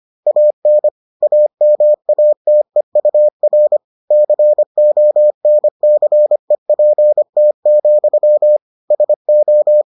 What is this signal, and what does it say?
Morse code, heard as a keyed tone, about 25 wpm, 600 Hz. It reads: AN AMATEUR CONCEPT, SO